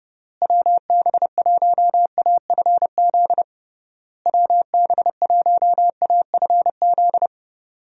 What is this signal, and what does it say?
Morse code, heard as a keyed tone, about 30 wpm, 700 Hz. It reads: WB1AFZ WB1AFZ
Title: USA Call Signs - 6 Characters - Repeated 3x - 30wpm